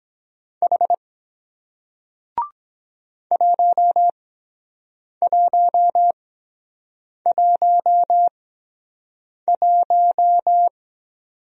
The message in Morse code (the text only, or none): H 1111